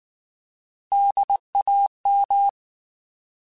Morse code, heard as a keyed tone, about 19 wpm, 800 Hz. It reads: DAM
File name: Alphabetical / Alphabetical mp3s / Ds @18